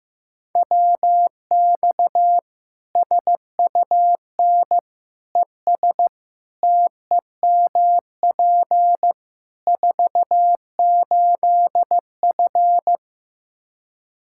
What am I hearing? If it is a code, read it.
Morse code, heard as a keyed tone, about 15 wpm, 700 Hz. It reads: WX SUN ES TEMP 48F
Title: K1USN SST QSO Practice 15wpm